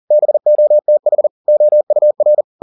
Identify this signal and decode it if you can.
Morse code, heard as a keyed tone, about 40 wpm, 600 Hz. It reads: BOTH OUR
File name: the old man and the sea 2 40wpm